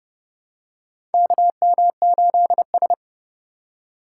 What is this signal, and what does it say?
Morse code, heard as a keyed tone, about 32 wpm, 700 Hz. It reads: KM8S
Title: USA Call Signs - 4 Characters - 6x Character Spacing - 30wpm